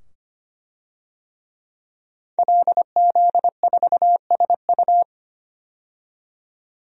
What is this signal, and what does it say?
Morse code, heard as a keyed tone, about 25 wpm, 700 Hz. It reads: LZ4SU
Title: DXCC-100 Call Signs - 5 Characters - 25wpm